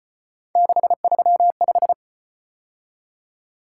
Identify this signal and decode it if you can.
Morse code, heard as a keyed tone, about 35 wpm, 700 Hz. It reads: 635